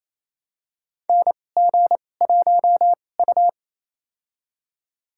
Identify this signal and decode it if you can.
Morse code, heard as a keyed tone, about 28 wpm, spaced as for 20 wpm, 700 Hz. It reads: NG1U
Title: USA Call Signs - 4 Characters - 2x Character Spacing - 28wpm